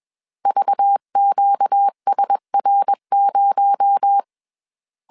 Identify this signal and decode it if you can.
Morse code, heard as a keyed tone, about 21 wpm, 800 Hz. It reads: VQSR0
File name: CW / DET / 1 6 1500C S C 20wpm 2116